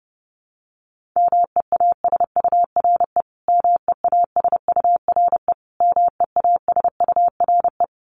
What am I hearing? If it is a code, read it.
Morse code, heard as a keyed tone, about 30 wpm, 700 Hz. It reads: MEASURE MEASURE MEASURE
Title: Top 500 Words Repeated 30wpm